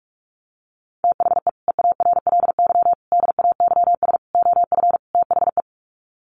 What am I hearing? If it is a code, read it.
Morse code, heard as a keyed tone, about 44 wpm, 700 Hz. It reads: THE EARLY DAYS OF THE
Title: Sets of 5 Words Encore 45wpm